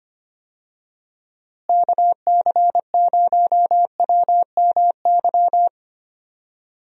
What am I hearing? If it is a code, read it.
Morse code, heard as a keyed tone, about 25 wpm, 700 Hz. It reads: KC0WMY
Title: USA Call Signs - 6 Characters - Repeated 2x - 25wpm